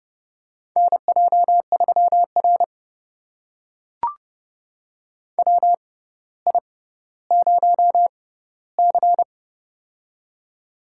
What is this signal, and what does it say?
Morse code, heard as a keyed tone, about 29 wpm, spaced as for 15 wpm, 700 Hz. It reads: NJ3R WI0C
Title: USA Call Signs - 4 Characters - 6x Character Spacing - 30wpm